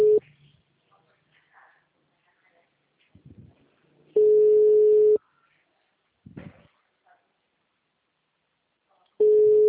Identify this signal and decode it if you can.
Ringback tone (Europe, 425 Hz), with speech.